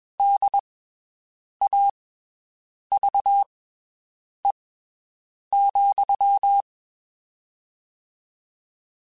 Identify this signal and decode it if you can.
Morse code, heard as a keyed tone, about 22 wpm, spaced as for 7 wpm, 800 Hz. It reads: DAVE,
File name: (spaced) / QSOs Spaced-mp3s / WD0BGP 1 @20 sp